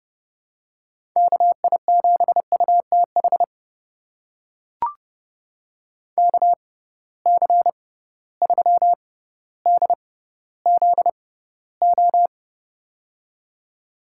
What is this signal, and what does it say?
Morse code, heard as a keyed tone, about 32 wpm, spaced as for 16 wpm, 700 Hz. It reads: KI7UTH KC3DZO
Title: USA Call Signs - 6 Characters - 6x Character Spacing - 30wpm